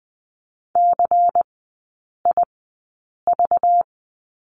Morse code, heard as a keyed tone, about 20 wpm, spaced as for 8 wpm, 690 Hz. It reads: CIV
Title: 205 Lesson 33e - Letter j - Three Characters 20wpm